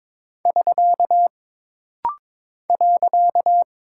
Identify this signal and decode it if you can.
Morse code, heard as a keyed tone, about 22 wpm, 700 Hz. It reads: <SK> .
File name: practice / 22wpm / Single Punctuation-Prosign - Rapid-Fire - 22wpm